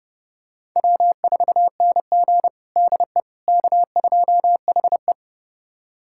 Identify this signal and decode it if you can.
Morse code, heard as a keyed tone, about 30 wpm, 700 Hz. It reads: W4NG DE K2HE
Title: USA Callsign DE Callsign - 4 Characters - 30wpm